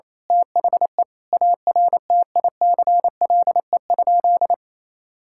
Morse code, heard as a keyed tone, about 28 wpm, 700 Hz. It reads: THE ARTICLE?
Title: Sentences from Top 1000 Words - Review Part 5 - 28wpm